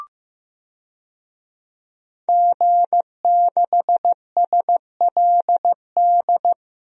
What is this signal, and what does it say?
Morse code, heard as a keyed tone, about 15 wpm, 700 Hz. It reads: G6SLD
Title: DXCC-100 Call Signs - 5 Characters - Repeated 3x - 15wpm